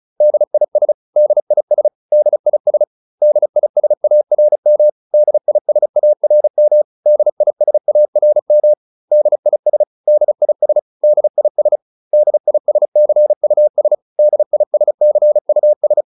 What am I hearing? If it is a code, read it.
Morse code, heard as a keyed tone, about 34 wpm, 600 Hz. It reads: DIS DIS DIS DISARM DISARM DISARM DIS DIS DIS DISCUS DISCUS
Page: advanced-practice-files